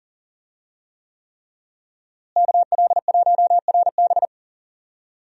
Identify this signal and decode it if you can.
Morse code, heard as a keyed tone, about 40 wpm, 700 Hz. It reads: KL1RB
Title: USA Call Signs - 5 Characters - Repeated 3x - 40wpm